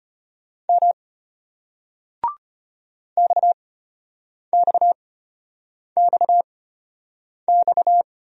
Morse code, heard as a keyed tone, about 35 wpm, spaced as for 8 wpm, 700 Hz. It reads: M XXXX